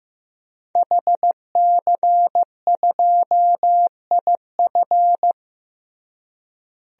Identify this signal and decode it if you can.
Morse code, heard as a keyed tone, about 16 wpm, 700 Hz. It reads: HC2IF
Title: DXCC-100 Call Signs - 5 Characters - Repeated 2x - 15wpm